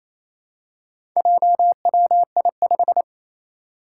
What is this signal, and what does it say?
Morse code, heard as a keyed tone, about 27 wpm, 700 Hz. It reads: JWI5
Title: Four Challenging Letter-Number Combinations 28wpm